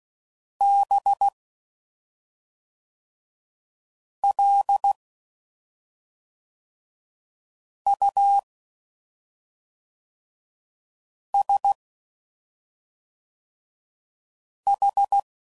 Morse code, heard as a keyed tone, about 16 wpm, 800 Hz. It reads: B L U S H
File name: Suffixes / Special ING's / SlowToFaster ING's